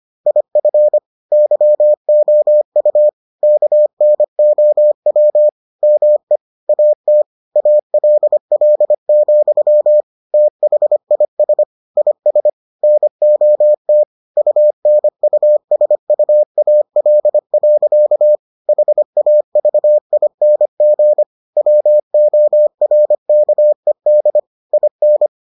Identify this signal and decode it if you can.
Morse code, heard as a keyed tone, about 25 wpm, 600 Hz. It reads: IF YOU KNOW ME AT ALL, THIS IS NOT UNUSUAL. HAVING WORKED IN